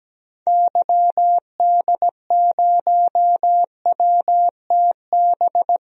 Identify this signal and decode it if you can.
Morse code, heard as a keyed tone, about 17 wpm, 700 Hz. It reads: YD0WTB
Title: DXCC-100 Call Signs - 6 Characters - 17wpm